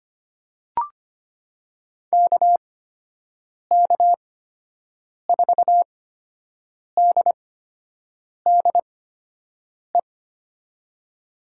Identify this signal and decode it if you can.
Morse code, heard as a keyed tone, about 25 wpm, spaced as for 7 wpm, 700 Hz. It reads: KK4DDE
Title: USA Call Signs - 8x Character Spacing - 25wpm